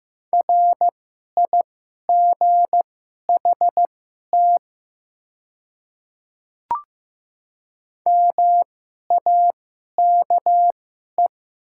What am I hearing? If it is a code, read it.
Morse code, heard as a keyed tone, about 15 wpm, spaced as for 11 wpm, 700 Hz. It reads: RIGHT MAKE
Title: 2x Character Spacing - Top 200 Words - 15wpm